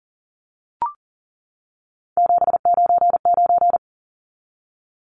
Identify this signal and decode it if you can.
Morse code, heard as a keyed tone, about 40 wpm, 700 Hz. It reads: 799
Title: Numbers - 3-Digit 40wpm